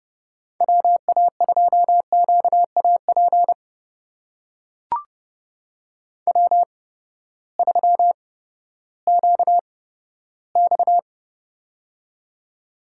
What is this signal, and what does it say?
Morse code, heard as a keyed tone, about 30 wpm, spaced as for 15 wpm, 700 Hz. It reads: WA2QAP W3QX
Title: USA Call Signs - 8x Character Spacing - 30wpm